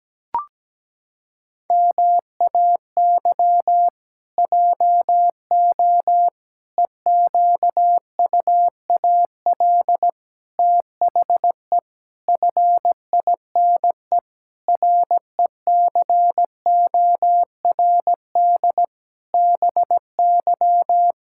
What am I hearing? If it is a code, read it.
Morse code, heard as a keyed tone, about 17 wpm, 700 Hz. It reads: MAY JO EQUAL THE FINE RECORD BY
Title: Pangram Challenge v2 - 17wpm